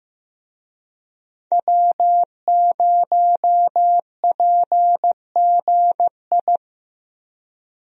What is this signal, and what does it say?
Morse code, heard as a keyed tone, about 15 wpm, 700 Hz. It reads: W0PGI
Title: USA Call Signs - 5 Characters - Repeated 3x - 15wpm